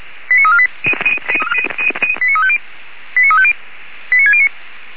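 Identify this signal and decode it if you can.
Five-tone selective call (ZVEI), heard as quick runs of five tones, tones of about 70 ms. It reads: 89348 89350 89350 89609
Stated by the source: Received with an SDR; ZVEI five-tone call 89347.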